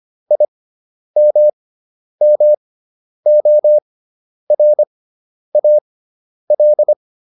Morse code, heard as a keyed tone, about 25 wpm, spaced as for 10 wpm, 600 Hz. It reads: IMMORAL